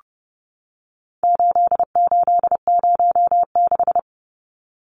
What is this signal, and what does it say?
Morse code, heard as a keyed tone, about 30 wpm, 700 Hz. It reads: 8806
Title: Numbers - 4-Digit 30wpm